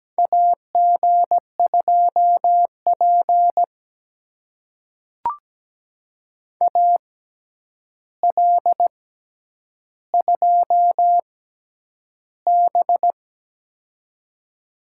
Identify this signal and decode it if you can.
Morse code, heard as a keyed tone, about 17 wpm, spaced as for 9 wpm, 700 Hz. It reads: AG2P AL2B
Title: USA Call Signs - 4 Characters - 6x Character Spacing - 17wpm